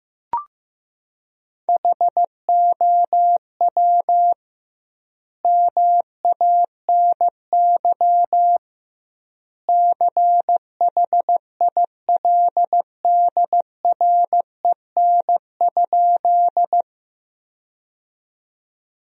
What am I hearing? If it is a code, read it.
Morse code, heard as a keyed tone, about 15 wpm, 700 Hz. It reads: HOW MANY CHILDREN?
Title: 2x Word-Spacing - Sentences from Top 300 Words - 15wpm